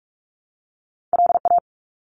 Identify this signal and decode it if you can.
Morse code, heard as a keyed tone, about 44 wpm, 700 Hz. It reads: LA
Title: ARRL Sections 45wpm